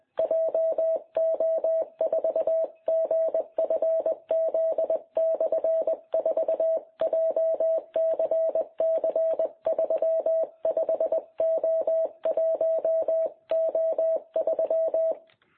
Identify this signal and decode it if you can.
Morse code, heard as a keyed tone, about 20 wpm, 660 Hz. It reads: JO4GFZ/4 JCC35O1O3
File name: jh8cbh / 2021cw